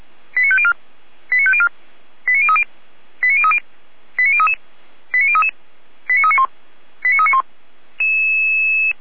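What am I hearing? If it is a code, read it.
Five-tone selective call (ZVEI), heard as quick runs of five tones, tones of about 70 ms. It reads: 89504 89504 89030 89030 89033 89033 89391 89391
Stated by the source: ZVEI five-tone call received by SDR.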